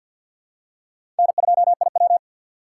Morse code, heard as a keyed tone, about 48 wpm, 700 Hz. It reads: N2IW
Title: USA Call Signs - 4 Characters - 2x Character Spacing - 50wpm